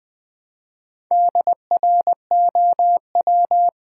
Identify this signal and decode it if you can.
Morse code, heard as a keyed tone, about 20 wpm, 700 Hz. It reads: DROW
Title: Build A Word - 1 Letter At A Time - 5-Letter Words - 20wpm